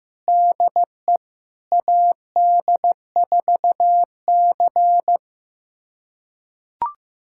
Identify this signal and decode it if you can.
Morse code, heard as a keyed tone, about 15 wpm, 700 Hz. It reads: DE AD4C
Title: USA Callsign DE Callsign - 4 Characters - 15wpm